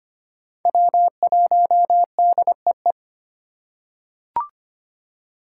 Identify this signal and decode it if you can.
Morse code, heard as a keyed tone, about 25 wpm, 700 Hz. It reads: W1DEE
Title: USA Call Signs - 5 Characters - 3x Character Spacing - 25wpm